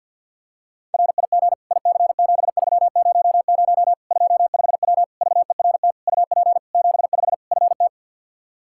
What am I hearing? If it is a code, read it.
Morse code, heard as a keyed tone, about 51 wpm, 700 Hz. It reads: RIG IC7300 25W VERT UP 75 FT
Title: Practice QSO - Exchange 2 - Info - 50wpm